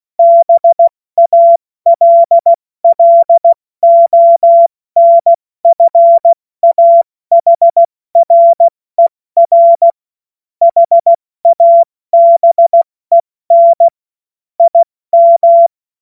Morse code, heard as a keyed tone, about 16 wpm, spaced as for 14 wpm, 680 Hz. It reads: BALLONFAHRER HABEN IM
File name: Morsekurs / Witze / Witz 16. August 70 F80